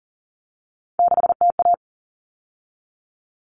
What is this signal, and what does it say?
Morse code, heard as a keyed tone, about 40 wpm, 700 Hz. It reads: 6TA
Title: All Cut Numbers - 3-Digit 40wpm